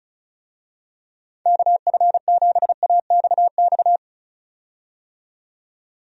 Morse code, heard as a keyed tone, about 34 wpm, 700 Hz. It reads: KF7AXX
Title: USA Call Signs - 6 Characters - 35wpm